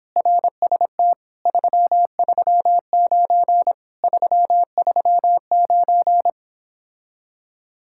Morse code, heard as a keyed tone, about 29 wpm, 700 Hz. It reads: RST 339 339